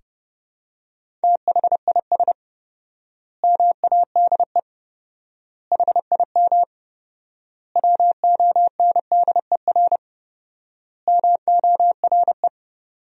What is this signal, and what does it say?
Morse code, heard as a keyed tone, about 30 wpm, 700 Hz. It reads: THIS MADE HIM WONDER MORE
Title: 4x Word-Spacing - Sentences from Top 500 Words - 30wpm